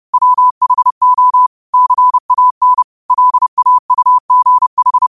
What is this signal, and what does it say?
Morse code, heard as a keyed tone, about 30 wpm, 1000 Hz. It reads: WHO CAN LAUGH